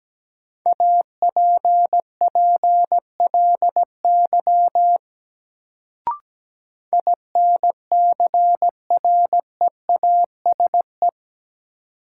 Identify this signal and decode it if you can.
Morse code, heard as a keyed tone, about 17 wpm, 700 Hz. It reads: APPLY INCREASE
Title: Sentences from Top 1200 Words - New Word Drill Alt - 17wpm